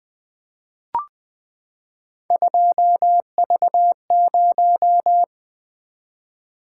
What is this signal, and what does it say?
Morse code, heard as a keyed tone, about 20 wpm, 700 Hz. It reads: 2V0
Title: Three Letter-Number Combinations - 20wpm